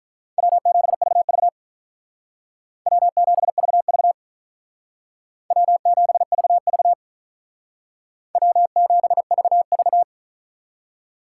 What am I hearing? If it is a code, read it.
Morse code, heard as a keyed tone, about 45 wpm, 700 Hz. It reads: W7VV W7VV W7VV W7VV